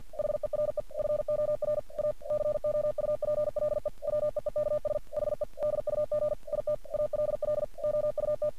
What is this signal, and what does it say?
Morse code, heard as a keyed tone, about 46 wpm, 600 Hz. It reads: HERE FOR A COUPLE WEEKS HE DUG IT ALL OUT
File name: archived / CRSnet